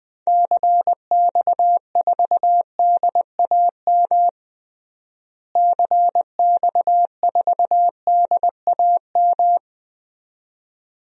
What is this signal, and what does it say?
Morse code, heard as a keyed tone, about 21 wpm, 700 Hz. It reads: CX4DAM CX4DAM
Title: DXCC-100 Call Signs - 6 Characters - Repeated 2x - 20wpm